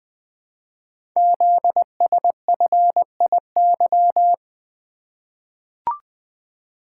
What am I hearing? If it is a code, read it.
Morse code, heard as a keyed tone, about 20 wpm, 700 Hz. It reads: ZSFIY
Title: Code Groups - 5 Letters - 20wpm